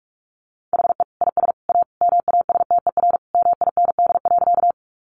Morse code, heard as a keyed tone, about 44 wpm, 700 Hz. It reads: HE IS A MASTER MIND.
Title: Sentences from Top 1300 Words 45wpm